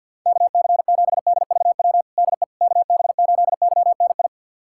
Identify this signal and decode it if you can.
Morse code, heard as a keyed tone, about 47 wpm, 700 Hz. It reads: KC7DVW DE KB8YNI
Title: USA Callsign DE Callsign 50wpm